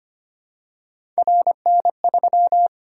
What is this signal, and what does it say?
Morse code, heard as a keyed tone, about 25 wpm, 700 Hz. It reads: RN3
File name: Three Letter-Number Combinations - 25wpm